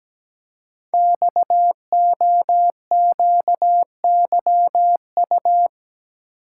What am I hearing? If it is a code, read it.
Morse code, heard as a keyed tone, about 17 wpm, 700 Hz. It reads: XOQYU
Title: Code Groups - 5 Letters - 17wpm